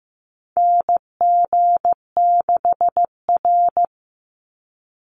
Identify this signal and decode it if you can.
Morse code, heard as a keyed tone, about 15 wpm, 700 Hz. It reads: NG6R
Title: NA SOTA Chaser Callsigns 15wpm